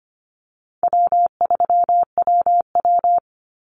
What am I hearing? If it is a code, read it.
Morse code, heard as a keyed tone, about 25 wpm, 700 Hz. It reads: W3WW
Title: Call Signs - 25wpm